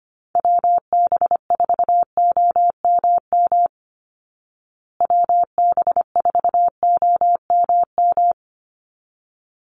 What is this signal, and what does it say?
Morse code, heard as a keyed tone, about 25 wpm, 700 Hz. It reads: WB4OMM WB4OMM
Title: Call Signs Repeated 3x - 25wpm